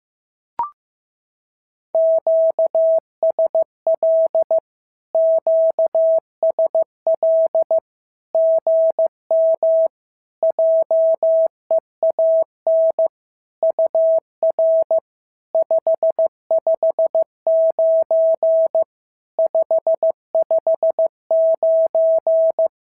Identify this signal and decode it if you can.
Morse code, heard as a keyed tone, about 15 wpm, 650 Hz. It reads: QSL QSL GM JEAN UR 559 559
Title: SOTA QSO Practice 15wpm